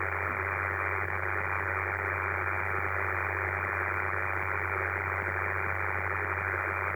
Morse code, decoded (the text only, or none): none